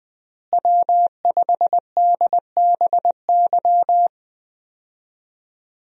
Morse code, heard as a keyed tone, about 21 wpm, 700 Hz. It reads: W5DBY
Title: USA Call Signs - 5 Characters - Repeated 2x - 20wpm